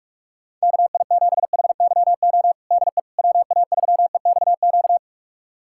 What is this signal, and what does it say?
Morse code, heard as a keyed tone, about 41 wpm, 700 Hz. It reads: KI7HYO DE WA3EXQ
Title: USA Callsign DE Callsign - 6 Characters - 45wpm